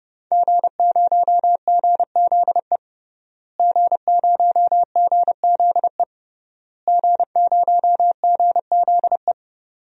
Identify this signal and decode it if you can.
Morse code, heard as a keyed tone, about 30 wpm, 700 Hz. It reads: G0GZE G0GZE G0GZE
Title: DXCC-100 Call Signs - 5 Characters - Repeated 3x - 30wpm